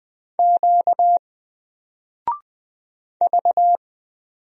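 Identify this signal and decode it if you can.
Morse code, heard as a keyed tone, about 20 wpm, 700 Hz. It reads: Q V
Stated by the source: Single letter and number Morse drill at 20 wpm, from N4LCD.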